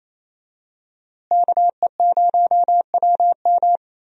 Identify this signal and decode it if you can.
Morse code, heard as a keyed tone, about 28 wpm, 700 Hz. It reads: KE0WM
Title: USA Call Signs - 5 Characters - Repeated 3x - 28wpm